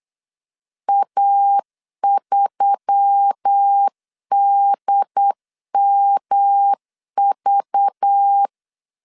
Morse code, heard as a keyed tone, about 8 wpm, 800 Hz. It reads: A3DMV
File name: 8wpm